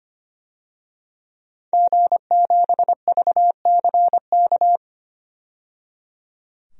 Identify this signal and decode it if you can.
Morse code, heard as a keyed tone, about 25 wpm, 700 Hz. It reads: G7VCK
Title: DXCC-100 Call Signs - 5 Characters - 25wpm